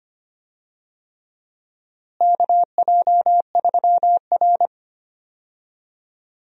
Morse code, heard as a keyed tone, about 25 wpm, 700 Hz. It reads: KJ3R
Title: USA Call Signs - 4 Characters - Repeated 3x - 25wpm